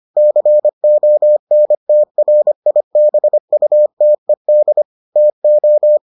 Morse code, heard as a keyed tone, about 25 wpm, 600 Hz. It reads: CONTRIBUTED TO